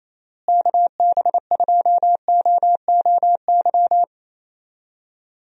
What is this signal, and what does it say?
Morse code, heard as a keyed tone, about 27 wpm, 700 Hz. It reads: KB2OOY
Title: USA Call Signs - 6 Characters - Repeated 2x - 28wpm